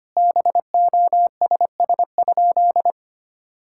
Morse code, heard as a keyed tone, about 25 wpm, 700 Hz. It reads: BOSS?
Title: Sentences from Top 1000 Words - Review Part 6 - 25wpm